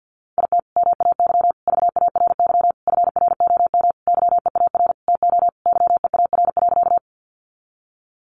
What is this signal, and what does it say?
Morse code, heard as a keyed tone, about 50 wpm, 700 Hz. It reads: IT MAY VARY FROM YEAR TO YEAR.
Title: Sentences from Top 1200 Words 50wpm